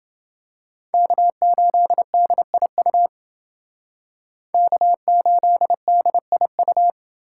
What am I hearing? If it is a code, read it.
Morse code, heard as a keyed tone, about 29 wpm, 700 Hz. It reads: K8DIU K8DIU